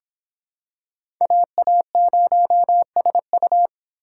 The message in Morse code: AA0SU

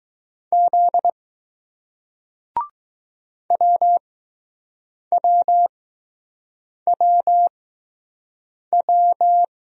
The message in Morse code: Z WWWW

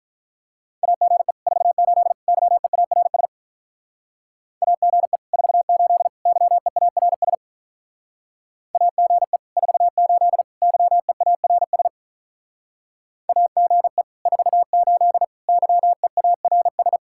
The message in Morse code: AGE 48 YEARS AGE 48 YEARS AGE 48 YEARS AGE 48 YEARS

